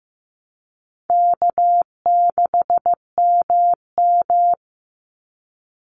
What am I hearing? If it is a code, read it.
Morse code, heard as a keyed tone, about 15 wpm, 700 Hz. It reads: K6MM
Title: Easy Call Signs Repeated 3x - 15wpm